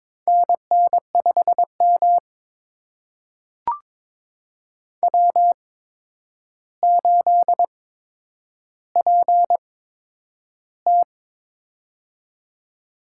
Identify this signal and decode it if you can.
Morse code, heard as a keyed tone, about 22 wpm, spaced as for 9 wpm, 700 Hz. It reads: NN5M W8PT
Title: USA Call Signs - 4 Characters - 8x Character Spacing - 22wpm